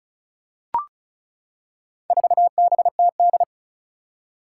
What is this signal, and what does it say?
Morse code, heard as a keyed tone, about 34 wpm, 700 Hz. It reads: 4BTD
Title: Four Challenging Letter-Number Combinations 35wpm